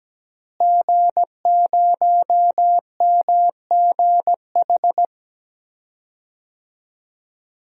Morse code, heard as a keyed tone, about 17 wpm, 700 Hz. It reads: G0MGH
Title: DXCC-100 Call Signs - 5 Characters - Repeated 2x - 17wpm